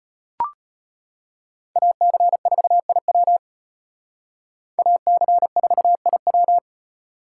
Morse code, heard as a keyed tone, about 32 wpm, 700 Hz. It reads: AC4IW AC4IW